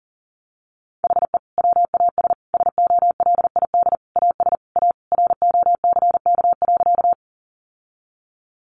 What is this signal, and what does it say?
Morse code, heard as a keyed tone, about 40 wpm, 700 Hz. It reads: HE WAS SOLID AS A ROCK.
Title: Sentences from Top 1500 Words 40wpm